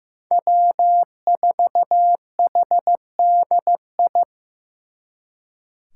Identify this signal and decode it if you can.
Morse code, heard as a keyed tone, about 17 wpm, 700 Hz. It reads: W4HDI
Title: Speed Racing - USA 1x3 Call Signs 15wpm